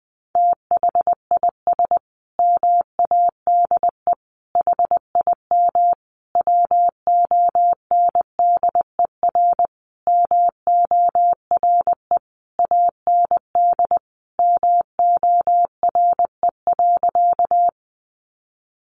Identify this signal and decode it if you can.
Morse code, heard as a keyed tone, about 20 wpm, 700 Hz. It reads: THIS MADE HIM WONDER MORE AND MORE.
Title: Sentences from Top 500 Words 20wpm